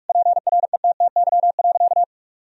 Morse code, heard as a keyed tone, about 46 wpm, 700 Hz. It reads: PRETTY.
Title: Sentences from Top 1000 Words - Review Part 1 - 45wpm